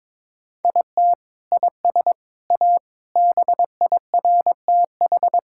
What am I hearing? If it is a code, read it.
Morse code, heard as a keyed tone, about 22 wpm, 700 Hz. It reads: IT IS A BIRTH
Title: Sentences from Top 1000 Words - Review Part 5 - 22wpm